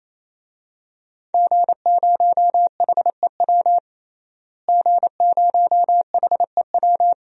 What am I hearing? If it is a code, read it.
Morse code, heard as a keyed tone, about 28 wpm, 690 Hz. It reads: G0HEW G0HEW
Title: DXCC-100 Call Signs - 5 Characters - Repeated 3x - 28wpm